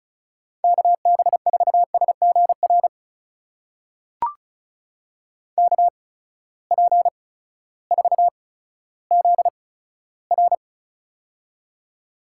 Morse code, heard as a keyed tone, about 36 wpm, spaced as for 16 wpm, 700 Hz. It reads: KB4SGR KP4ZR
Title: USA Call Signs - 8x Character Spacing - 35wpm